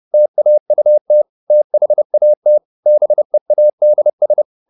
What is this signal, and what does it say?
Morse code, heard as a keyed tone, about 30 wpm, 600 Hz. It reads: TAUT THAT BEADS
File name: the old man and the sea / the old man and the sea 1 30wpm